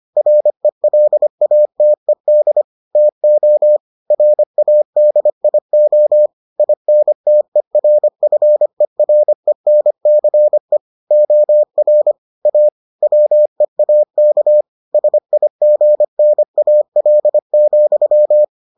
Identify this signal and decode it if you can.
Morse code, heard as a keyed tone, about 25 wpm, 600 Hz. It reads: RELATED TO RADIO INTERFERENCE OR A WEAK SIGNAL,